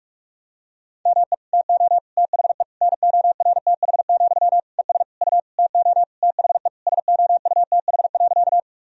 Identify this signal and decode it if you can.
Morse code, heard as a keyed tone, about 41 wpm, 700 Hz. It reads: ME TO THE NORTH, ES U TO THE SOUTH.
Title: Sentences from Top 400 Words and Abbreviations 45wpm